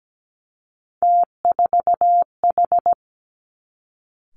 Morse code, heard as a keyed tone, about 17 wpm, 700 Hz. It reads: T4H
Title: Three Challenging Letter-Number Combinations 17wpm